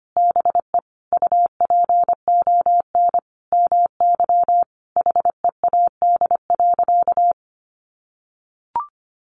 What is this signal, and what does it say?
Morse code, heard as a keyed tone, about 25 wpm, 700 Hz. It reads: BE UPON MY HEAD.
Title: Sentences from Top 1100 Words 25wpm